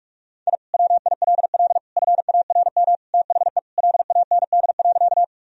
Morse code, heard as a keyed tone, about 41 wpm, 700 Hz. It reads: I WILL FARM THE LAND.
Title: Sentences from Top 1000 Words - Review Part 2 - 45wpm